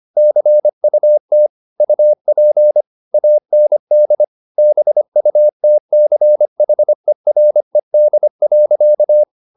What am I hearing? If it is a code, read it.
Morse code, heard as a keyed tone, about 25 wpm, 600 Hz. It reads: CUT UP AND BUTCHERED.
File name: the old man and the sea part 1 25wpm